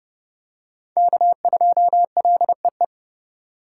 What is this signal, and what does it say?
Morse code, heard as a keyed tone, about 28 wpm, 700 Hz. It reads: K2LEE